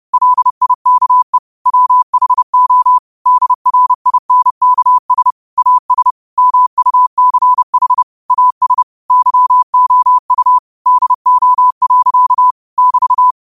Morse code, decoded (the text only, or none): LIKE WHO DRINKS AS MUCH AS YOU DO. <BT>